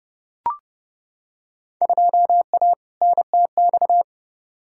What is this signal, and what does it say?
Morse code, heard as a keyed tone, about 31 wpm, 700 Hz. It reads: 2A NTX
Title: Field Day Exchanges - 30wpm